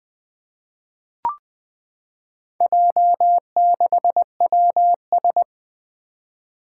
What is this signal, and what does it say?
Morse code, heard as a keyed tone, about 20 wpm, 700 Hz. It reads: J6WS
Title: Four Challenging Letter-Number Combinations 20wpm